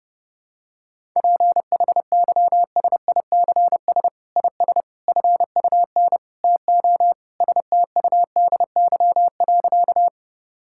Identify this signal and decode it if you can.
Morse code, heard as a keyed tone, about 30 wpm, 700 Hz. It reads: PHYSICS IS FUN TO STUDY.
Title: Sentences from Top 1000 Words - Review Part 5 - 30wpm